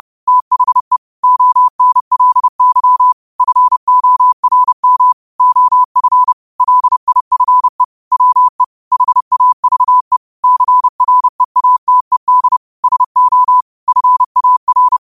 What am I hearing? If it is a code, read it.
Morse code, heard as a keyed tone, about 30 wpm, 1000 Hz. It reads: THE ONLY FORM OF LIFE WE HAVE CREATED SO FAR
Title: 30 WPM morse code quotes for Fri, 01 Mar 2024